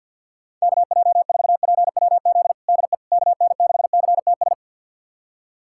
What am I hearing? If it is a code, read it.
Morse code, heard as a keyed tone, about 45 wpm, 700 Hz. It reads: KJ4PWZ DE KN6CTS
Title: USA Callsign DE Callsign 50wpm